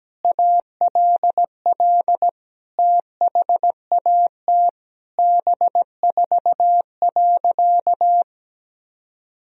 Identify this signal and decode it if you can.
Morse code, heard as a keyed tone, about 17 wpm, 700 Hz. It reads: ALL THAT B4.
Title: Sentences from Top 400 Words and Abbreviations 17wpm